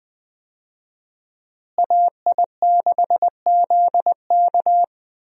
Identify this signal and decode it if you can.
Morse code, heard as a keyed tone, about 21 wpm, 700 Hz. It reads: AI6ZK
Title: USA Call Signs - 5 Characters - Repeated 3x - 20wpm